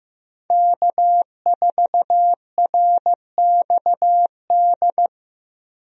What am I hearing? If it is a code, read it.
Morse code, heard as a keyed tone, about 15 wpm, 700 Hz. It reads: K4RXD
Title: USA Call Signs - 5 Characters - 15wpm